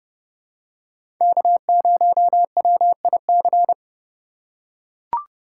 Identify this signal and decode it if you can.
Morse code, heard as a keyed tone, about 30 wpm, 700 Hz. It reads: K0WIC
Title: USA Call Signs - 5 Characters - 3x Character Spacing - 30wpm